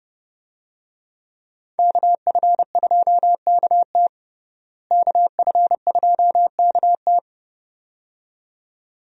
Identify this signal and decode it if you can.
Morse code, heard as a keyed tone, about 30 wpm, 700 Hz. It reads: KF2KT KF2KT
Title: USA Call Signs - 5 Characters - Repeated 2x - 30wpm